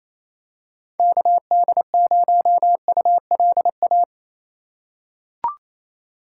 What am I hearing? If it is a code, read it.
Morse code, heard as a keyed tone, about 27 wpm, 700 Hz. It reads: KD0ULA
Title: USA Call Signs - 2x Character Spacing - 28wpm